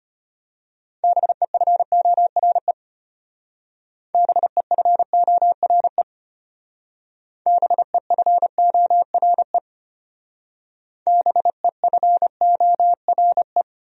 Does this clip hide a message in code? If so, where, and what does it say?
Yes: Morse code from 1.0 to 13.6 s, heard as a keyed tone, about 32 wpm, 700 Hz. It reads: BEFORE BEFORE BEFORE BEFORE